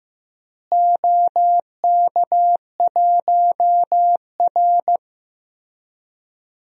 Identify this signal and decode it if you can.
Morse code, heard as a keyed tone, about 15 wpm, 700 Hz. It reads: OK1R
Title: DXCC-100 Call Signs - 4 Characters - Repeated 2x - 15wpm